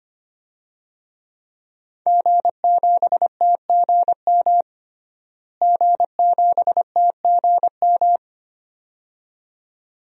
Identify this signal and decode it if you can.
Morse code, heard as a keyed tone, about 25 wpm, 700 Hz. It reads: G7TGM G7TGM
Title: DXCC-100 Call Signs - 5 Characters - Repeated 2x - 25wpm